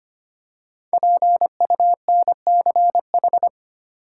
Morse code, heard as a keyed tone, about 25 wpm, 700 Hz. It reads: PUNCH
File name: 5 Letter Words - First 3 Letters Given - 25wpm